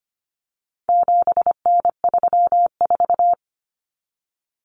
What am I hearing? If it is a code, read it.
Morse code, heard as a keyed tone, about 25 wpm, 700 Hz. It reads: 7N34
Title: Common Cut Numbers - 4-Digit 25wpm